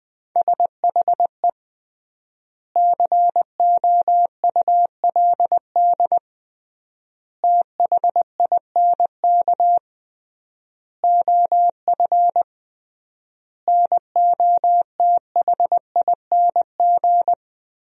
Morse code, heard as a keyed tone, about 20 wpm, 700 Hz. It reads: SHE COULD THINK OF NOTHING
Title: 3x Word-Spacing - Sentences from Top 500 Words - 20wpm